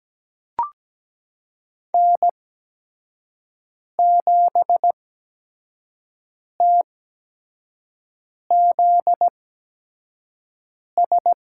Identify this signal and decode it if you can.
Morse code, heard as a keyed tone, about 17 wpm, spaced as for 5 wpm, 700 Hz. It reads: N7TZS
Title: USA Call Signs - 5 Characters - 8x Character Spacing - 17wpm